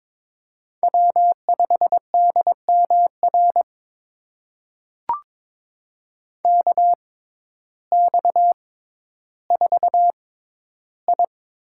Morse code, heard as a keyed tone, about 22 wpm, spaced as for 12 wpm, 700 Hz. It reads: W5DMR KX4I
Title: USA Call Signs - 5 Characters - 6x Character Spacing - 22wpm